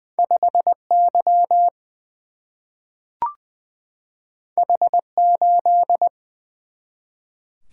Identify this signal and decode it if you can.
Morse code, heard as a keyed tone, about 20 wpm, 700 Hz. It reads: 5 Y H 8
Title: Single-Letter Single-Number 20wpm